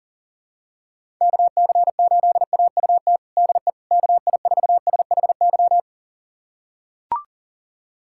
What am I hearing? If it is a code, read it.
Morse code, heard as a keyed tone, about 40 wpm, 700 Hz. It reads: KC8AUT DE KI4SHY
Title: USA Callsign DE Callsign - 6 Characters - 40wpm